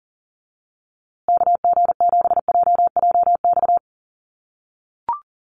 Morse code, heard as a keyed tone, about 40 wpm, 700 Hz. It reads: KG7JJX
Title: USA Call Signs Repeated 2x 40wpm